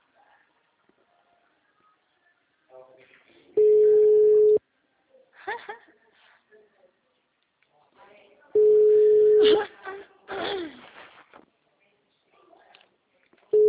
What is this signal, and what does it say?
Ringback tone (Europe, 425 Hz), with speech.
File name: out- -2002- - - .8371